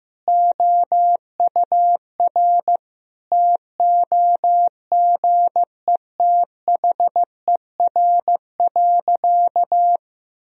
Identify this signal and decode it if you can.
Morse code, heard as a keyed tone, about 15 wpm, 700 Hz. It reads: OUR TOGETHER.
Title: Sentences from Top 1000 Words - Review Part 1 - 15wpm